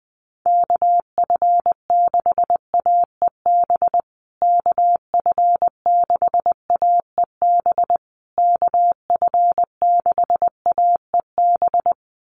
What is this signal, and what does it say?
Morse code, heard as a keyed tone, about 20 wpm, 700 Hz. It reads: KF6AEB KF6AEB KF6AEB
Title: USA Call Signs Repeated 3x 20wpm